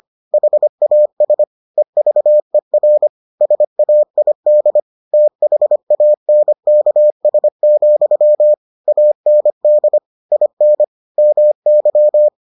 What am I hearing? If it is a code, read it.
Morse code, heard as a keyed tone, about 25 wpm, 600 Hz. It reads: HAS EVER SAID THANKS, AND IN MY